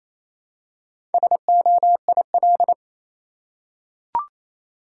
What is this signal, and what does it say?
Morse code, heard as a keyed tone, about 28 wpm, 700 Hz. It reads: SOIL
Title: Four Letter Words - 28wpm